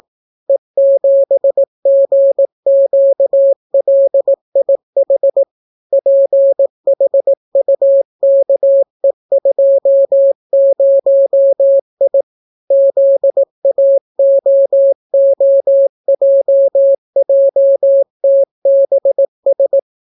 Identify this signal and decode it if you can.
Morse code, heard as a keyed tone, about 18 wpm, 550 Hz. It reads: E7GQLIH PHUKE20I ZAOOJJTBS